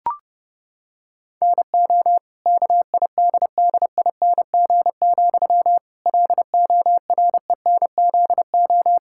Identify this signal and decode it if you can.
Morse code, heard as a keyed tone, about 28 wpm, 700 Hz. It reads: NO KIDDING, LORENZO
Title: Pangram Challenge v2 - 30wpm